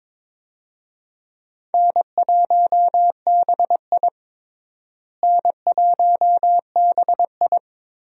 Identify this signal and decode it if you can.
Morse code, heard as a keyed tone, about 22 wpm, 700 Hz. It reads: N1BI N1BI
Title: USA Call Signs - 4 Characters - Repeated 2x - 22wpm